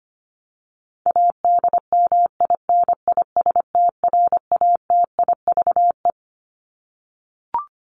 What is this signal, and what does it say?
Morse code, heard as a keyed tone, about 25 wpm, 700 Hz. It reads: ADMINISTRATIVE
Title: Top 1501-1750 Words 25wpm